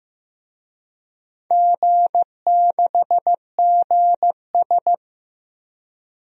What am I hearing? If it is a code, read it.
Morse code, heard as a keyed tone, about 15 wpm, 700 Hz. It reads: G6GS
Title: DXCC-100 Call Signs - 4 Characters - 15wpm